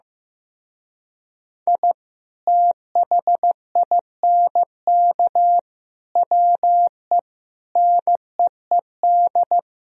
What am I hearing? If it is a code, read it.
Morse code, heard as a keyed tone, about 15 wpm, 700 Hz. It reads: I THINK WE NEED
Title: Speed-Racing - Sets of 5 Words 15wpm